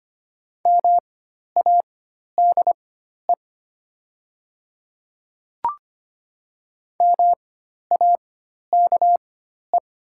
Morse code, heard as a keyed tone, about 25 wpm, spaced as for 12 wpm, 700 Hz. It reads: MADE MAKE
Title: 4x Character Spacing - Top 200 Words - 25wpm